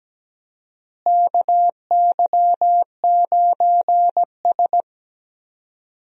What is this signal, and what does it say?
Morse code, heard as a keyed tone, about 17 wpm, 700 Hz. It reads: KY9S
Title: USA Call Signs - 4 Characters - Repeated 2x - 17wpm